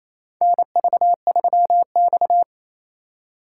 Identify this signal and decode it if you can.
Morse code, heard as a keyed tone, about 28 wpm, 700 Hz. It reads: NV3X
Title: USA Call Signs - 4 Characters - 2x Character Spacing - 28wpm